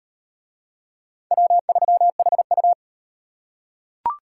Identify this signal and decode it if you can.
Morse code, heard as a keyed tone, about 35 wpm, 700 Hz. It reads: W3HU